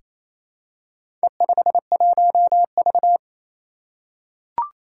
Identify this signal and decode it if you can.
Morse code, heard as a keyed tone, about 28 wpm, 700 Hz. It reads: E51V
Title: Four Challenging Letter-Number Combinations 28wpm